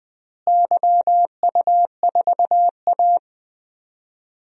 Morse code, heard as a keyed tone, about 20 wpm, 700 Hz. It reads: YU4A
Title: DXCC-100 Call Signs - 4 Characters - Repeated 3x - 20wpm